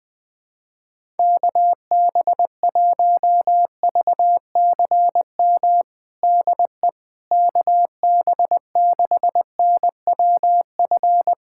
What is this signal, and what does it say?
Morse code, heard as a keyed tone, about 20 wpm, 700 Hz. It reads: KB1VCM DE KB6NWF
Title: USA Callsign DE Callsign - 6 Characters - 20wpm